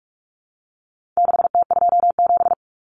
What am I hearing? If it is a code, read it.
Morse code, heard as a keyed tone, about 44 wpm, 700 Hz. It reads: 6T27
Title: Common Cut Numbers - 4-Digit 45wpm